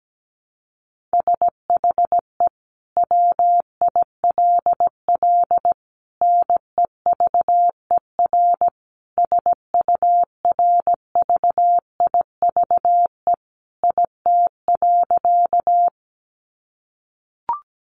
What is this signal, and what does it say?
Morse code, heard as a keyed tone, about 17 wpm, 700 Hz. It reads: SHE WILL NEVER SURVIVE IT.
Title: Sentences from Top 1300 Words 17wpm